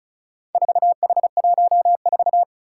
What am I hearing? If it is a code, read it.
Morse code, heard as a keyed tone, about 36 wpm, 700 Hz. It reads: 4H14
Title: Four Challenging Letter-Number Combinations 35wpm